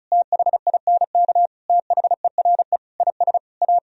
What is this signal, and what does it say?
Morse code, heard as a keyed tone, about 43 wpm, 700 Hz. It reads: THINK THERE IS A
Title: Speed-Racing - Sets of 5 Words 35wpm